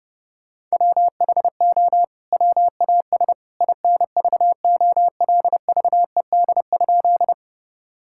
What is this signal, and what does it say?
Morse code, heard as a keyed tone, about 32 wpm, 700 Hz. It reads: WHO WAS INVOLVED?
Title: Sentences from Top 1000 Words - Review Part 5 - 30wpm